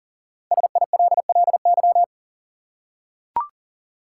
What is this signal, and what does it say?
Morse code, heard as a keyed tone, about 40 wpm, 700 Hz. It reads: SILLY
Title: Five Letter Words - 40wpm